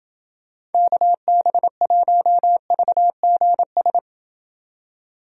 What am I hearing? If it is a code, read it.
Morse code, heard as a keyed tone, about 29 wpm, 700 Hz. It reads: KB1VGS